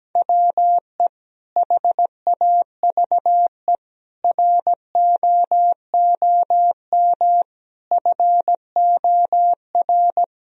Morse code, heard as a keyed tone, about 17 wpm, 700 Hz. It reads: WE HAVE ROOM FOR
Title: Sentences from Top 1000 Words - Review Part 1 - 17wpm